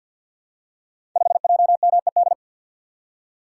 Morse code, heard as a keyed tone, about 51 wpm, 700 Hz. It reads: 5JMED
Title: Code Groups - 5 Letter:Numbers - 50wpm